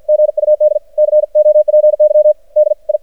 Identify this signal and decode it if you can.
Morse code, heard as a keyed tone, about 52 wpm, 610 Hz. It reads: GUD COPY DI
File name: archived / CRSnet